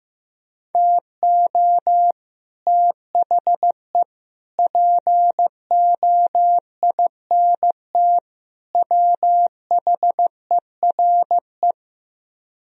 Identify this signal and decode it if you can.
Morse code, heard as a keyed tone, about 15 wpm, 700 Hz. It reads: TO THE POINT WHERE
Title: Speed-Racing - Sets of 4 Words 15wpm